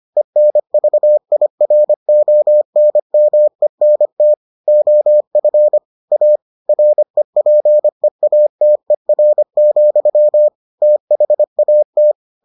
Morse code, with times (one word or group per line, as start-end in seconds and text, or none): 0.2-4.3 ENVIRONMENT
4.7-5.8 OF
6.1-6.4 A
6.7-10.5 REPEATER,
10.8-12.1 THAT